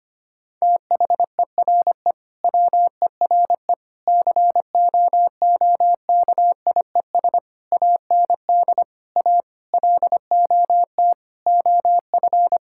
Morse code, heard as a keyed tone, about 25 wpm, 700 Hz. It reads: THERE WERE COOKIES AND A LOT OF
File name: practice / 25wpm / Sentences with 4-Word Phrases - 25wpm